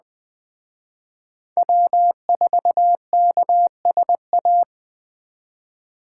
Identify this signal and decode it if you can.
Morse code, heard as a keyed tone, about 20 wpm, 700 Hz. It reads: W4KSA
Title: Speed Racing - USA 1x3 Call Signs 17wpm